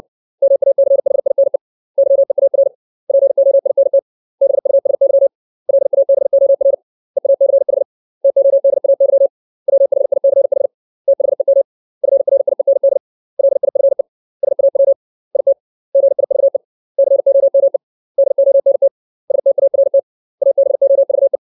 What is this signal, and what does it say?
Morse code, heard as a keyed tone, about 60 wpm, 550 Hz. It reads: MACHINE YEAR POINT BUSY LABOR EACH TODAY WHICH THEM FRIEND LIFE SAW IT GIVE COME DONT START ABOVE
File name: 632 words / 632 Words 60WPM